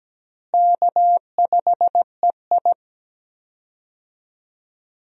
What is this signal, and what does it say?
Morse code, heard as a keyed tone, about 17 wpm, 700 Hz. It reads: K5EI
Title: USA Call Signs - 4 Characters - Repeated 3x - 17wpm